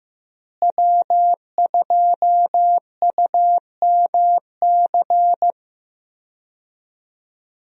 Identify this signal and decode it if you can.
Morse code, heard as a keyed tone, about 15 wpm, 700 Hz. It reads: W2UMC
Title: USA Call Signs - 5 Characters - Repeated 3x - 15wpm